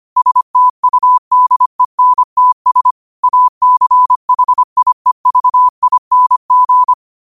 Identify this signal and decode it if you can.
Morse code, heard as a keyed tone, about 25 wpm, 1000 Hz. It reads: STUDENTS ACHIEVING
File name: qotdfiles / 25wpm